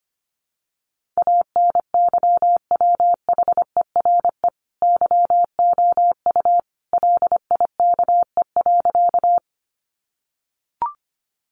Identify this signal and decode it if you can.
Morse code, heard as a keyed tone, about 25 wpm, 700 Hz. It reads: ANYWHERE YOU LIKE.
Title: Sentences from Top 1400 Words 25wpm